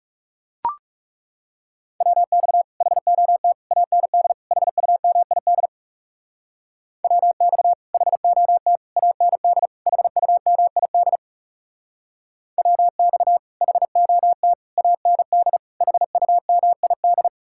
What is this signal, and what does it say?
Morse code, heard as a keyed tone, about 37 wpm, 700 Hz. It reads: WX HOT AND HUMID WX HOT AND HUMID WX HOT AND HUMID
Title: Speed-Racing Common QSO Phrases Encore - 30wpm